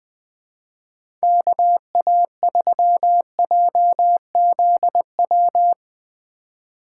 Morse code, heard as a keyed tone, about 20 wpm, 700 Hz. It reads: KA3JZW
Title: USA Call Signs - 6 Characters - 3x Character Spacing - 20wpm